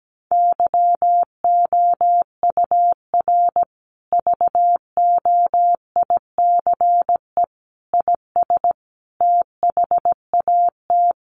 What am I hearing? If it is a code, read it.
Morse code, heard as a keyed tone, about 17 wpm, 700 Hz. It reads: YOUR VOICE IS THAT
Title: Sentences from Top 500 Words 17wpm